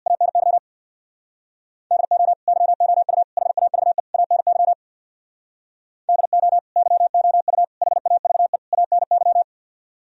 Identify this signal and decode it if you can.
Morse code, heard as a keyed tone, about 55 wpm, 700 Hz. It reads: ANY DO YOU HAVE ANY DO YOU HAVE ANY
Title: Speed-Racing - Sets of 4 Words 50wpm